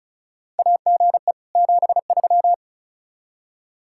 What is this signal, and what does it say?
Morse code, heard as a keyed tone, about 34 wpm, 700 Hz. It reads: AGE 73
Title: K1USN SST QSO Practice 35wpm